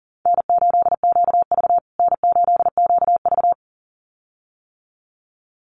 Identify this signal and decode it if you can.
Morse code, heard as a keyed tone, about 40 wpm, 700 Hz. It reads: N8QV N8QV
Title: USA Call Signs Repeated 2x 40wpm